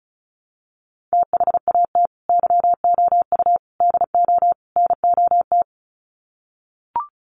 Morse code, heard as a keyed tone, about 34 wpm, 700 Hz. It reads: THAT YOU DO NOT
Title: Sets of 4 Words 35wpm